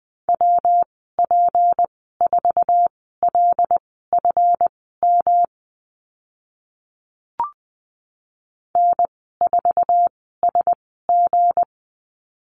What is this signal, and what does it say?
Morse code, heard as a keyed tone, about 20 wpm, spaced as for 14 wpm, 700 Hz. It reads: WP4LFM N4SG
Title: 167 Lesson 27g - Number 7 - Call Signs 20wpm